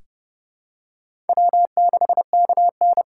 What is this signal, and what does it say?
Morse code, heard as a keyed tone, about 32 wpm, 700 Hz. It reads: W6KN
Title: USA Call Signs - 4 Characters - 2x Character Spacing - 30wpm